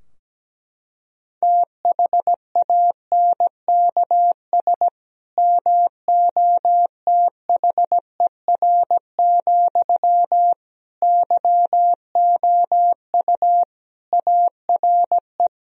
Morse code, heard as a keyed tone, about 17 wpm, 700 Hz. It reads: THANKS MOTHER, YOU ARE
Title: Sentences from Top 1000 Words - Review Part 3 - 17wpm